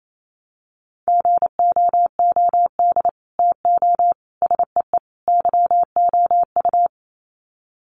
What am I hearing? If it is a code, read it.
Morse code, heard as a keyed tone, about 28 wpm, 700 Hz. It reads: GOOD TO SEE YOU
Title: Sets of 4 Words 28wpm